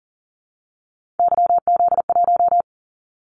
Morse code, heard as a keyed tone, about 38 wpm, 700 Hz. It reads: YZ1
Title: Callsign Prefixes for Top 25 DXCC - 40wpm